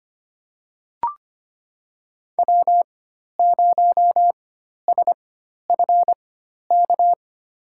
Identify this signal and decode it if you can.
Morse code, heard as a keyed tone, about 25 wpm, spaced as for 12 wpm, 700 Hz. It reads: W0SFK
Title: USA Call Signs - 5 Characters - 4x Character Spacing - 25wpm